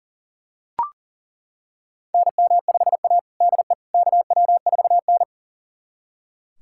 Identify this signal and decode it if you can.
Morse code, heard as a keyed tone, about 40 wpm, 700 Hz. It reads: NM5A DE KW4N
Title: USA Callsign DE Callsign - 4 Characters - 40wpm